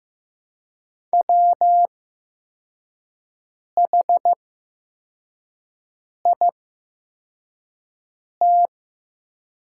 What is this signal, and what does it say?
Morse code, heard as a keyed tone, about 15 wpm, spaced as for 4 wpm, 700 Hz. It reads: WHIT
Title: 8x Character Spacing - Top 300 Words - 15wpm